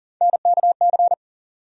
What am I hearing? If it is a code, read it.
Morse code, heard as a keyed tone, about 40 wpm, 700 Hz. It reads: NKC
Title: USA Call Sign Postfixes - 3 Letters 40wpm